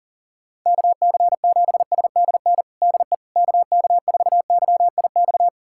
Morse code, heard as a keyed tone, about 40 wpm, 700 Hz. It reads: KC7SDN DE KK4YIX
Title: USA Callsign DE Callsign - 6 Characters - 40wpm